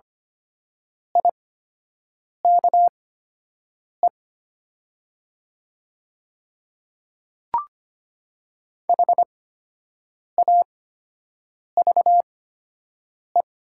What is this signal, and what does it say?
Morse code, heard as a keyed tone, about 25 wpm, spaced as for 7 wpm, 700 Hz. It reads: IKE HAVE
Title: 8x Character Spacing - Top 100 Words - 25wpm